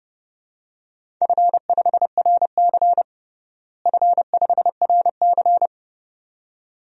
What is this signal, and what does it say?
Morse code, heard as a keyed tone, about 30 wpm, 700 Hz. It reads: F5RC F5RC
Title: DXCC-100 Call Signs - 4 Characters - Repeated 2x - 30wpm